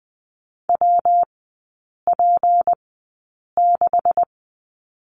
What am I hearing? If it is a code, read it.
Morse code, heard as a keyed tone, about 20 wpm, spaced as for 8 wpm, 690 Hz. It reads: WP6
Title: 159 Lesson 26e - Letter g - Three Characters 20wpm